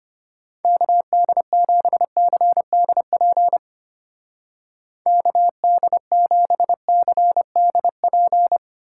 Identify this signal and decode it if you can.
Morse code, heard as a keyed tone, about 32 wpm, 700 Hz. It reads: KD7CDP KD7CDP